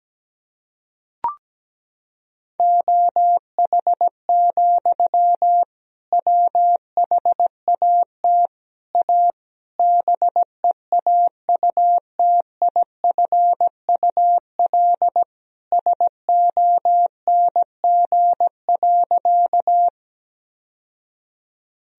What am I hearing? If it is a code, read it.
Morse code, heard as a keyed tone, about 17 wpm, 700 Hz. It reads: OH, WHAT A BEAUTIFUL SONG.
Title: Sentences from Top 1000 Words - Review Part 2 - 17wpm